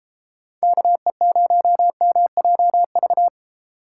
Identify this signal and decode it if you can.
Morse code, heard as a keyed tone, about 31 wpm, 700 Hz. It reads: KE0MJV